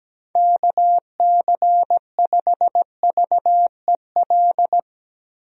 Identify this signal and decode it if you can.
Morse code, heard as a keyed tone, about 17 wpm, 700 Hz. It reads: KC5VEL
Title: USA Call Signs - 6 Characters - Repeated 3x - 17wpm